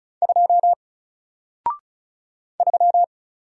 Morse code, heard as a keyed tone, about 36 wpm, 700 Hz. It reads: 2 3
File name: Single-Letter-Number-35wpm